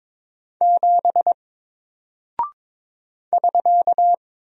Morse code, heard as a keyed tone, about 22 wpm, 700 Hz. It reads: 7 <SK>
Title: Single Character (Letter:Number:Punctuation:Prosign) Alt 22wpm